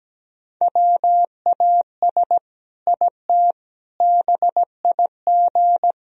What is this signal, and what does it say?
Morse code, heard as a keyed tone, about 17 wpm, 700 Hz. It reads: WAS IT BIG
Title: Sentences from Top 1000 Words - Review Part 1 - 17wpm